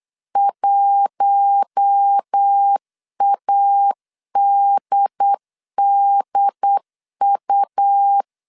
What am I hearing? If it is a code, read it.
Morse code, heard as a keyed tone, about 8 wpm, 800 Hz. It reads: 1ADDU
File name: 8wpm